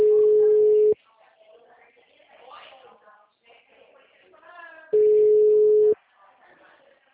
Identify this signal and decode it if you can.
Ringback tone (Europe, 425 Hz), with speech.